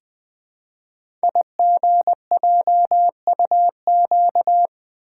Morse code, heard as a keyed tone, about 20 wpm, 700 Hz. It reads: IGJUQ
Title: Code Groups - 5 Letters - 20wpm